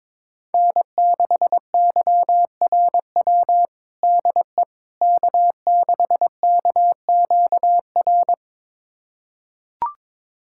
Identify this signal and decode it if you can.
Morse code, heard as a keyed tone, about 22 wpm, 700 Hz. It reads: N6YRW DE K6KQR
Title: USA Callsign DE Callsign - 5 Characters - 22wpm